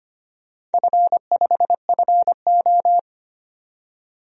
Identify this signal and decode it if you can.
Morse code, heard as a keyed tone, about 25 wpm, 700 Hz. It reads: F5FO
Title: DXCC-100 Call Signs - 4 Characters - 25wpm